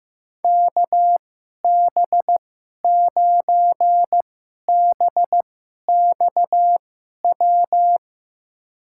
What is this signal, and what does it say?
Morse code, heard as a keyed tone, about 15 wpm, spaced as for 11 wpm, 700 Hz. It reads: KB9BXW
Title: USA Call Signs - 6 Characters - 2x Character Spacing - 15wpm